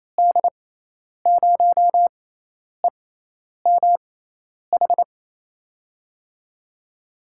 Morse code, heard as a keyed tone, about 28 wpm, spaced as for 10 wpm, 700 Hz. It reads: D0EMH
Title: USA Call Signs - 6 Characters - 6x Character Spacing - 28wpm